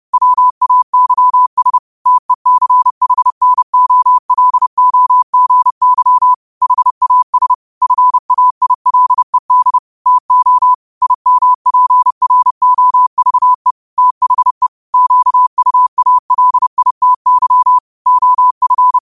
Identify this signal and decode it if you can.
Morse code, heard as a keyed tone, about 30 wpm, 1000 Hz. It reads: WAYS TECHNOLOGY HAS FAILED TO IMPROVE THE QUALITY OF